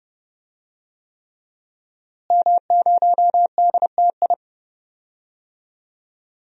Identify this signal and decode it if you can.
Morse code, heard as a keyed tone, about 30 wpm, 700 Hz. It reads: M0DTI
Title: DXCC-100 Call Signs - 5 Characters - Repeated 3x - 30wpm